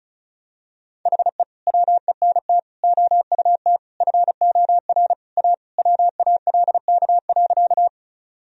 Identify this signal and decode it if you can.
Morse code, heard as a keyed tone, about 35 wpm, 700 Hz. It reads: HE WENT OUT FOR A WALK.
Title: Sentences from Top 1000 Words - Review Part 1 - 35wpm